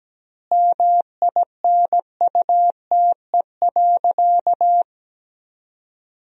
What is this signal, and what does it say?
Morse code, heard as a keyed tone, about 17 wpm, 700 Hz. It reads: MINUTE.
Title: 3x Word-Spacing - Sentences from Top 500 Words - 17wpm